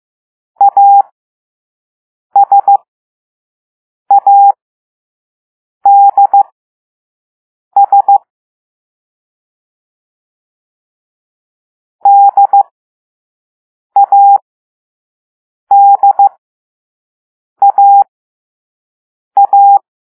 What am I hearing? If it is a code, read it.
Morse code, heard as a keyed tone, about 15 wpm, spaced as for 5 wpm, 800 Hz. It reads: ASADS DADAA
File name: Morse / Level3B